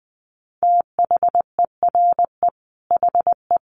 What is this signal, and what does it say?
Morse code, heard as a keyed tone, about 20 wpm, 700 Hz. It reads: THERE HE
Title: Sentences from Top 100 Words 20wpm